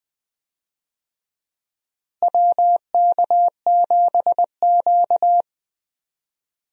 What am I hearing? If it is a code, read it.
Morse code, heard as a keyed tone, about 20 wpm, 700 Hz. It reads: WK7Q
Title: USA Call Signs - 4 Characters - Repeated 3x - 20wpm